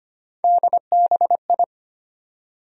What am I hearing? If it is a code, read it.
Morse code, heard as a keyed tone, about 25 wpm, 700 Hz. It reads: DBI
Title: USA Call Sign Postfixes - 3 Letters 25wpm